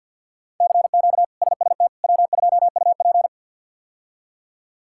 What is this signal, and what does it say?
Morse code, heard as a keyed tone, about 52 wpm, 700 Hz. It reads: CQ SST W2UP
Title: K1USN SST QSO Practice - 50wpm